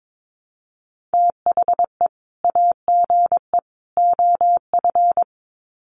Morse code, heard as a keyed tone, about 22 wpm, 700 Hz. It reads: THE AGE OF
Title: Sets of 3 Words 22wpm